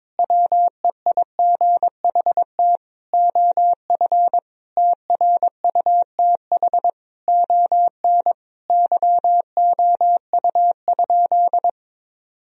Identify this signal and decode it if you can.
Morse code, heard as a keyed tone, about 22 wpm, 700 Hz. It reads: WEIGHT OF TRUTH ON YOU?
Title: Sentences from Top 1000 Words - Review Part 3 - 22wpm